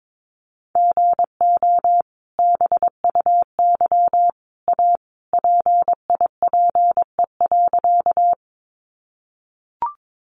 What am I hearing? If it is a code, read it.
Morse code, heard as a keyed tone, about 22 wpm, 700 Hz. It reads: GO BUY A PIPE.
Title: Sentences from Top 1500 Words 22wpm